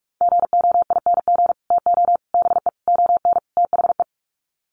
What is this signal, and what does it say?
Morse code, heard as a keyed tone, about 44 wpm, 700 Hz. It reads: GOING TO BE ON THE
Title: Sets of 5 Words Encore2 45wpm